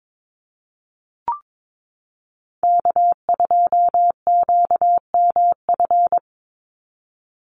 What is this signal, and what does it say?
Morse code, heard as a keyed tone, about 22 wpm, 700 Hz. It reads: K2QMF
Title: Call Signs Repeated 2x - 22wpm